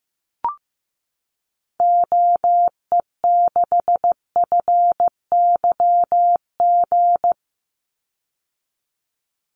Morse code, heard as a keyed tone, about 15 wpm, 700 Hz. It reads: OE6FYG
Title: Call Signs Repeated 3x - 15wpm